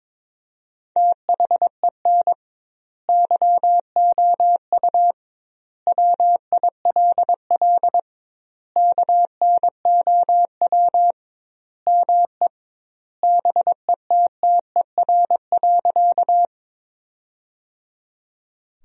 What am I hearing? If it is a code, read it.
Morse code, heard as a keyed tone, about 22 wpm, 700 Hz. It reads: THEN YOU WILL KNOW ME BETTER.
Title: 2x Word-Spacing - Sentences from Top 400 Words - 22wpm